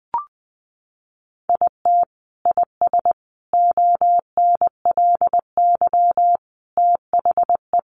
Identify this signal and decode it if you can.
Morse code, heard as a keyed tone, about 20 wpm, 700 Hz. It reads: IT IS ONLY THE
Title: Sentences from Top 800 Words 20wpm